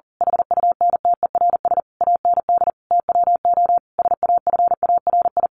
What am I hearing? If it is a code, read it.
Morse code, heard as a keyed tone, about 40 wpm, 700 Hz. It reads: HUNTERS AND TWO SAFARI
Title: A Sound Of Thunder 40wpm